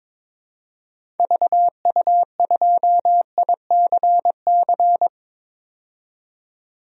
Morse code, heard as a keyed tone, about 22 wpm, 700 Hz. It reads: VU2ICC
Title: DXCC-100 Call Signs - 6 Characters - 22wpm